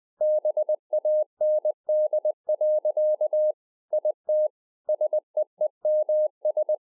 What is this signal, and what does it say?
Morse code, heard as a keyed tone, about 20 wpm, 610 Hz. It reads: BAND. IT SEEMS
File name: 2011-04-mp3 / A SMALL TRANCEIVER 3 20WPM